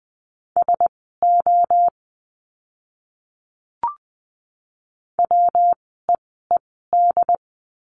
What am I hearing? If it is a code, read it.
Morse code, heard as a keyed tone, about 20 wpm, spaced as for 15 wpm, 700 Hz. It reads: SO WEED
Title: 115 Lesson 19f - Letter w - Words 20wpm